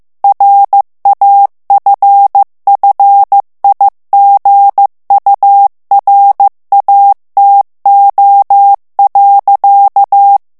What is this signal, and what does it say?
Morse code, heard as a keyed tone, about 15 wpm, 800 Hz. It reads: RAFFIGURATO.
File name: ITALIEN texte clair 1 vitesse 14 mots minute